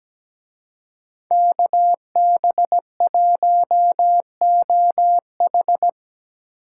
Morse code, heard as a keyed tone, about 20 wpm, 700 Hz. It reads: KB1OH
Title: Speed Racing - USA 2x2 Call Signs 17wpm